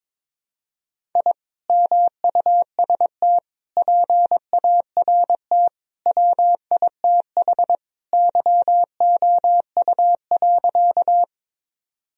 Morse code, heard as a keyed tone, about 22 wpm, 700 Hz. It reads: I MUST PART WITH YOU.
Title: Sentences from Top 1000 Words - Review Part 1 - 22wpm